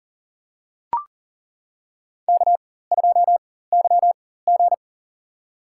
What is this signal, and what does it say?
Morse code, heard as a keyed tone, about 40 wpm, spaced as for 19 wpm, 700 Hz. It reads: K2YG
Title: USA Call Signs - 4x Character Spacing - 40wpm